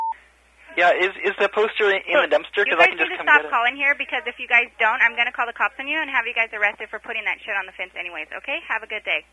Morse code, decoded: none